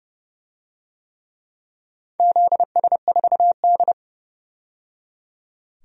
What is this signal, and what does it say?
Morse code, heard as a keyed tone, about 30 wpm, 700 Hz. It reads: ZS4D
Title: DXCC-100 Call Signs - 4 Characters - 30wpm